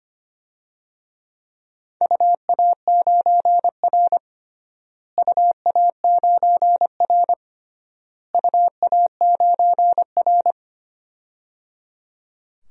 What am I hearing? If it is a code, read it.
Morse code, heard as a keyed tone, about 25 wpm, 700 Hz. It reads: UA9R UA9R UA9R
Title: DXCC-100 Call Signs - 4 Characters - Repeated 3x - 25wpm